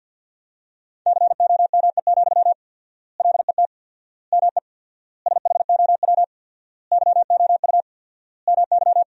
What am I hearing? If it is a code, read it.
Morse code, heard as a keyed tone, about 50 wpm, 700 Hz. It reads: COME, LET ME SHOW YOU MY
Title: 4x Word-Spacing - Sentences from Top 500 Words - 50wpm